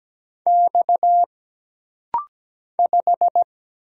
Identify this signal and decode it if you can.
Morse code, heard as a keyed tone, about 17 wpm, 700 Hz. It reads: X 5
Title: Single Letter-Number Alt - Rapid-Fire 17wpm